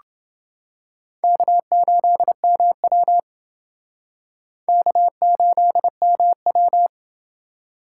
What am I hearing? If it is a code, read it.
Morse code, heard as a keyed tone, about 28 wpm, 700 Hz. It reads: K8MW K8MW